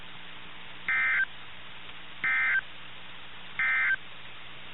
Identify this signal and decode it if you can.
Emergency Alert System (SAME) tones, heard as harsh buzzing bursts. It is heard from 0.8 to 4.0 s.
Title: NOAA Weather Radio - KIH61